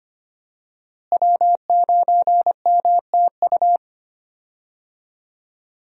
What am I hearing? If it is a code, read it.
Morse code, heard as a keyed tone, about 25 wpm, 700 Hz. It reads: W9MTU
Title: USA Call Signs - 5 Characters - Repeated 3x - 25wpm